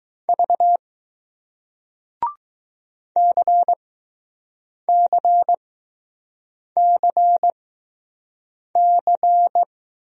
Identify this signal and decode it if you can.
Morse code, heard as a keyed tone, about 20 wpm, spaced as for 6 wpm, 700 Hz. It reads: V CCCC